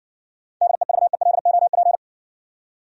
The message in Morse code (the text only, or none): DEVELOP